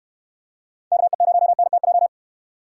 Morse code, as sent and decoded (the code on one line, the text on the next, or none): -.- . ----- -. - .---
KE0NTJ